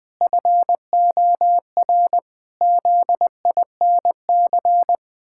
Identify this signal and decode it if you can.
Morse code, heard as a keyed tone, about 20 wpm, 700 Hz. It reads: FOR ZINC
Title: Pangram Challenge v2 - 20wpm